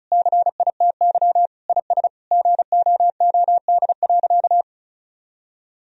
Morse code, heard as a keyed tone, about 36 wpm, 700 Hz. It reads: CITY IS GOOD.
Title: Sentences from Top 1000 Words - Review Part 4 - 35wpm